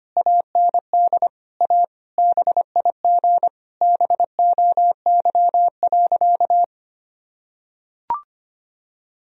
Morse code, heard as a keyed tone, about 25 wpm, 700 Hz. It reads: AND A BIG BOY.
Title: Sentences from Top 1000 Words - Review Part 1 - 25wpm